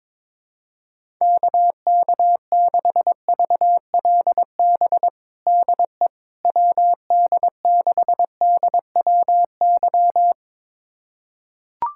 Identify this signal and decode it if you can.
Morse code, heard as a keyed tone, about 22 wpm, 700 Hz. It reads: KK6VLB DE WD6DWY
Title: USA Callsign DE Callsign - 6 Characters - 22wpm